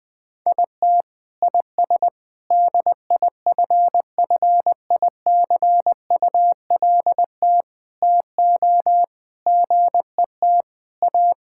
Morse code, heard as a keyed tone, about 20 wpm, 700 Hz. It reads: IT IS DIFFICULT TO GET A
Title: Sets of 9 Words - 20wpm